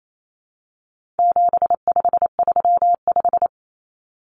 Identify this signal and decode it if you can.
Morse code, heard as a keyed tone, about 28 wpm, 700 Hz. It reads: 7535
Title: Common Cut Numbers - 4-Digit 28wpm